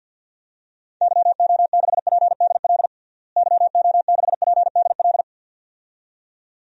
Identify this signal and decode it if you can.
Morse code, heard as a keyed tone, about 52 wpm, 700 Hz. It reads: YO6PDL YO6PDL
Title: DXCC-100 Call Signs - 6 Characters - Repeated 2x - 50wpm